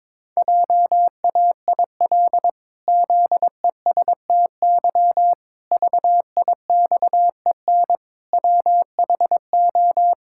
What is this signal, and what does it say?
Morse code, heard as a keyed tone, about 22 wpm, 700 Hz. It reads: JAIL ZESTY VIXEN WHO
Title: Pangram Challenge v2 - 22wpm